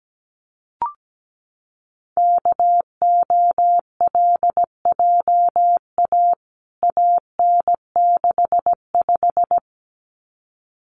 Morse code, heard as a keyed tone, about 17 wpm, 700 Hz. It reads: KOLJA AN65
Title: CWT Exchange 17wpm